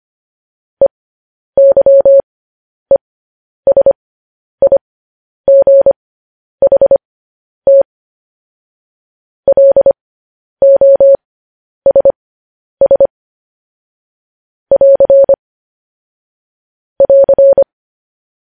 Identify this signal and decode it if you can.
Morse code, heard as a keyed tone, about 25 wpm, spaced as for 10 wpm, 550 Hz. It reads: EYESIGHT LOSS <AR> <AR>
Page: news headlines in morse code at 10 wpm